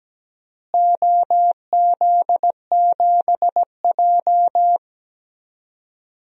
Morse code, heard as a keyed tone, about 17 wpm, 700 Hz. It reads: OZ7J
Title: DXCC-100 Call Signs - 4 Characters - Repeated 2x - 17wpm